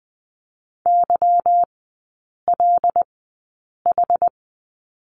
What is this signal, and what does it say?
Morse code, heard as a keyed tone, about 20 wpm, spaced as for 8 wpm, 690 Hz. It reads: YLH
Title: 205 Lesson 33e - Letter j - Three Characters 20wpm